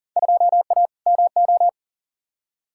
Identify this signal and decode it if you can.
Morse code, heard as a keyed tone, about 40 wpm, 700 Hz. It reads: 2A MO
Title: Field Day Exchanges Encore - 40wpm